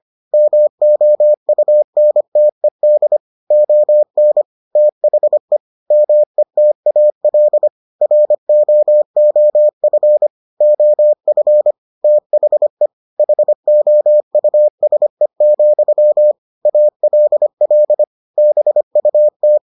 Morse code, heard as a keyed tone, about 25 wpm, 600 Hz. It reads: MOUNTED ON THE METAL ROOF OF THE HOUSE, ALL BUT